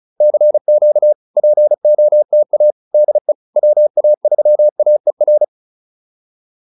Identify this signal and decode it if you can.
Morse code, heard as a keyed tone, about 35 wpm, 600 Hz. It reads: CQ POTA DE WA3AER